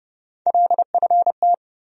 Morse code, heard as a keyed tone, about 30 wpm, 700 Hz. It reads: LFT
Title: USA Call Sign Postfixes - 3 Letters 30wpm